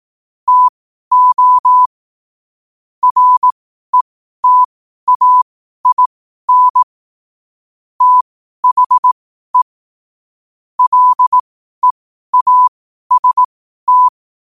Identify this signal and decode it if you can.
Morse code, heard as a keyed tone, about 18 wpm, spaced as for 12 wpm, 1000 Hz. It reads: TO RETAIN THE LEAST
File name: qotdfiles / 12wpm